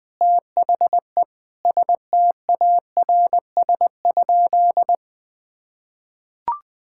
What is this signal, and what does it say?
Morse code, heard as a keyed tone, about 20 wpm, 700 Hz. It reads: THE STARS?
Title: Difficult Sentences with Hints - 20wpm